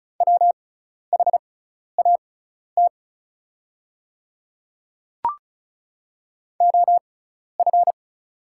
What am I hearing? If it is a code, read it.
Morse code, heard as a keyed tone, about 36 wpm, spaced as for 12 wpm, 700 Hz. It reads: WHAT OF
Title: 6x Character Spacing - Top 300 Words - 35wpm